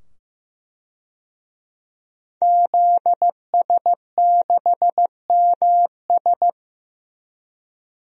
Morse code, heard as a keyed tone, about 15 wpm, 700 Hz. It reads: ZS6MS
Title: DXCC-100 Call Signs - 5 Characters - 15wpm